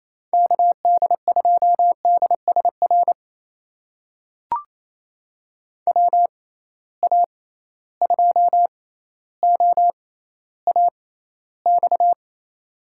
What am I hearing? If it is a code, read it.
Morse code, heard as a keyed tone, about 27 wpm, spaced as for 14 wpm, 700 Hz. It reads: KD2DSR WA2OAX
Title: USA Call Signs - 6 Characters - 6x Character Spacing - 28wpm